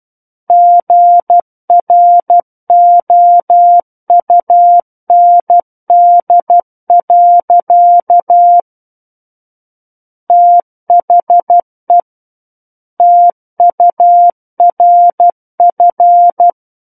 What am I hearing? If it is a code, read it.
Morse code, heard as a keyed tone, about 12 wpm, 700 Hz. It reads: GROUND. THE TURF
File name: Book One / 03-Chapter 3 - 12 WPM